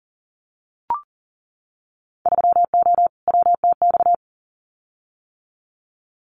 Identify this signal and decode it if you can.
Morse code, heard as a keyed tone, about 39 wpm, 700 Hz. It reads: 3O WTX
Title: Winter Field Day Exchanges Encore 40wpm